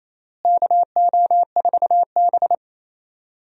Morse code, heard as a keyed tone, about 28 wpm, 700 Hz. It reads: KO4B
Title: USA Call Signs - 4 Characters - Repeated 3x - 28wpm